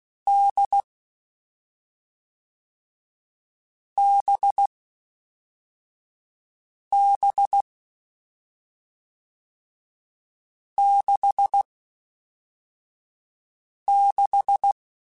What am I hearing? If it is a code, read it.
Morse code, heard as a keyed tone, about 16 wpm, 800 Hz. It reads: D B B 6 6
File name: Difficult Combos / Difficult Combos-mp3s / awj1 ndb6 gz7 @5